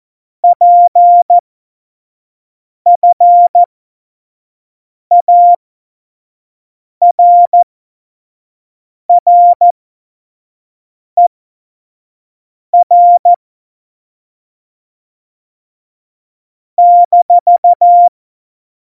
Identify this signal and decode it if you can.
Morse code, heard as a keyed tone, about 14 wpm, spaced as for 5 wpm, 700 Hz. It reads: PFARRER -
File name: Morsekurs / Witze / Witz 22. Dezember 2020, 25 F70